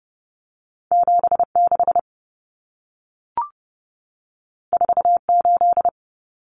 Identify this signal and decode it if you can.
Morse code, heard as a keyed tone, about 30 wpm, 700 Hz. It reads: 76 48
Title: Common Cut Numbers - 2-Digit 30wpm